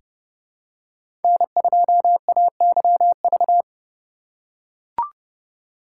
Morse code, heard as a keyed tone, about 32 wpm, 700 Hz. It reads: N2AYV
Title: USA Call Signs - 5 Characters - 3x Character Spacing - 30wpm